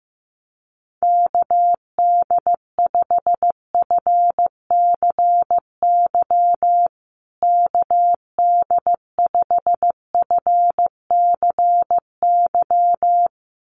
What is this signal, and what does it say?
Morse code, heard as a keyed tone, about 15 wpm, 700 Hz. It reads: KD5FCY KD5FCY
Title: USA Call Signs Repeated 2x 15wpm